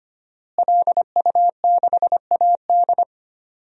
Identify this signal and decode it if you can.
Morse code, heard as a keyed tone, about 25 wpm, 700 Hz. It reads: LU6AD
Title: DXCC-100 Call Signs - 5 Characters - Repeated 3x - 25wpm